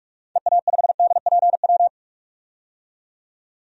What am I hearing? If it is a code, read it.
Morse code, heard as a keyed tone, about 42 wpm, 700 Hz. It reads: EA5DPW
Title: DXCC-100 Call Signs - 6 Characters - 45wpm